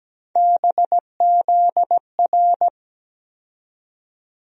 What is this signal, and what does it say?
Morse code, heard as a keyed tone, about 17 wpm, 700 Hz. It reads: BZR
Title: USA Call Sign Postfixes - 3 Letters 17wpm